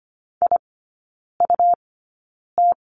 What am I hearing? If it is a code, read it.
Morse code, heard as a keyed tone, about 25 wpm, spaced as for 9 wpm, 700 Hz. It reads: IUT
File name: course / 25:10wpm / 100 Lesson 17e - Letter u - Three Characters 25wpm